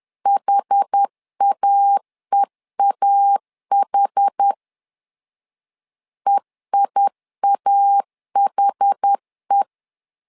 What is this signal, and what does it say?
Morse code, heard as a keyed tone, about 10 wpm, 800 Hz. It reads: HAEAH EIAHE
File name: LEZIONI / 2 UHDN / 2 UH 10wpm 1714